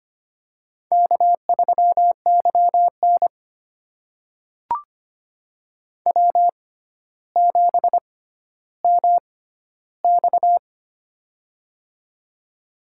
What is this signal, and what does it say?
Morse code, heard as a keyed tone, about 25 wpm, spaced as for 13 wpm, 700 Hz. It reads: K3YN W7MX
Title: USA Call Signs - 4 Characters - 6x Character Spacing - 25wpm